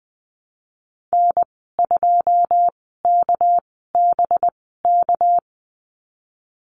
Morse code, heard as a keyed tone, about 20 wpm, spaced as for 14 wpm, 700 Hz. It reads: N2KBK
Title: 245 Lesson 39g - Letter z - Call Signs 20wpm